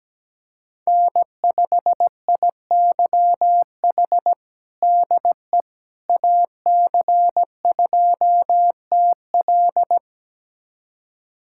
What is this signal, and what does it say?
Morse code, heard as a keyed tone, about 17 wpm, 700 Hz. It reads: N5IYH DE AC2TL
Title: USA Callsign DE Callsign - 5 Characters - 17wpm